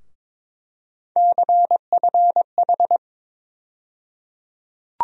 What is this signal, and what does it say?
Morse code, heard as a keyed tone, about 22 wpm, 700 Hz. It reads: CFH
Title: USA Call Sign Postfixes - 3 Letters 22wpm